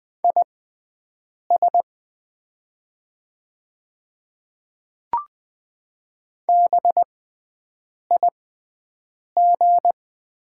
Morse code, heard as a keyed tone, about 20 wpm, spaced as for 7 wpm, 700 Hz. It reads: IS BIG
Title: 6x Character Spacing - Top 200 Words - 20wpm